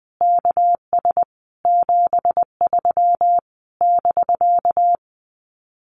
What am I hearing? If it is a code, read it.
Morse code, heard as a keyed tone, about 19 wpm, 700 Hz. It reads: KS 73 <BK>
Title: Common QSO Phrases Encore 20wpm